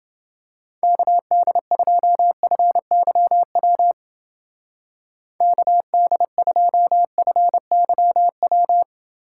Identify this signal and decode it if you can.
Morse code, heard as a keyed tone, about 29 wpm, 700 Hz. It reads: KD2FYW KD2FYW